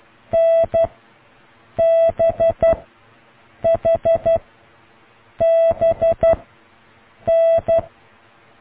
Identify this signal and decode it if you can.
Morse code, heard as a keyed tone, about 12 wpm, 660 Hz. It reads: N B H B N